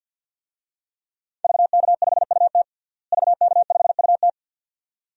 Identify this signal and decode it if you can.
Morse code, heard as a keyed tone, about 48 wpm, 700 Hz. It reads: VK5UT VK5UT
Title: DXCC-100 Call Signs - 5 Characters - Repeated 2x - 50wpm